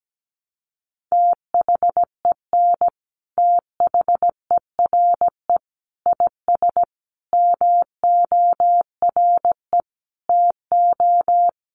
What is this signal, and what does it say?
Morse code, heard as a keyed tone, about 17 wpm, 700 Hz. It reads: THEN THERE IS MORE TO
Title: Sentences from Top 100 Words 17wpm